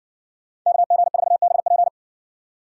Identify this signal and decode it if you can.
Morse code, heard as a keyed tone, about 65 wpm, 700 Hz. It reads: KG4BP